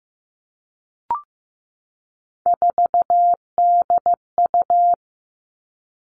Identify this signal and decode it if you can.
Morse code, heard as a keyed tone, about 15 wpm, 700 Hz. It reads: 4DU
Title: All Cut Numbers - 3-Digit 15wpm